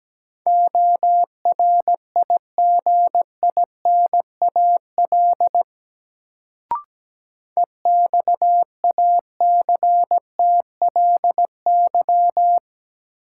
Morse code, heard as a keyed tone, about 17 wpm, 700 Hz. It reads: ORIGINAL EXACTLY
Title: Sentences from Top 900 Words - New Word Drill - 17wpm